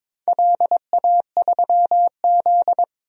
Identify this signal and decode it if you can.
Morse code, heard as a keyed tone, about 23 wpm, 700 Hz. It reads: LA3Z
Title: DXCC-100 Call Signs - 4 Characters - 22wpm